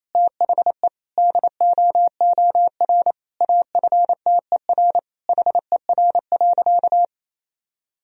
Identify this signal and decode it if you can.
Morse code, heard as a keyed tone, about 28 wpm, 700 Hz. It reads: THE DOOR AFTER HER.
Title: Sentences from Top 1000 Words - Review Part 1 - 28wpm